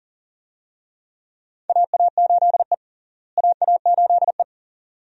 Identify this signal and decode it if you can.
Morse code, heard as a keyed tone, about 40 wpm, 700 Hz. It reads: AA8E AA8E
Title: USA Call Signs - 4 Characters - Repeated 2x - 40wpm